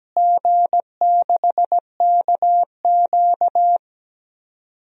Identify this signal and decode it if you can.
Morse code, heard as a keyed tone, about 17 wpm, 700 Hz. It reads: G6KQ
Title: DXCC-100 Call Signs - 4 Characters - Repeated 3x - 17wpm